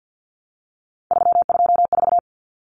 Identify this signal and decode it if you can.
Morse code, heard as a keyed tone, about 50 wpm, 700 Hz. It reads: 324
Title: Common Cut Numbers - 3-Digit 50wpm